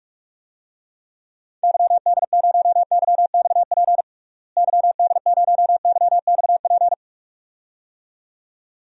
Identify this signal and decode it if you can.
Morse code, heard as a keyed tone, about 46 wpm, 700 Hz. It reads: YD0YXP YD0YXP
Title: DXCC-100 Call Signs - 6 Characters - Repeated 2x - 45wpm